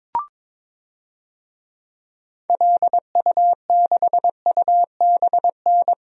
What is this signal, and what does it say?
Morse code, heard as a keyed tone, about 21 wpm, 700 Hz. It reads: LU6UBN
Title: DXCC-100 Call Signs - 6 Characters - 22wpm